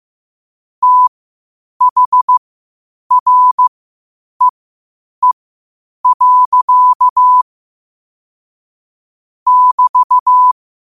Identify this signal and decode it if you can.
Morse code, heard as a keyed tone, about 15 wpm, spaced as for 8 wpm, 1000 Hz. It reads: THREE. <BT>
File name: qotdfiles / 8wpm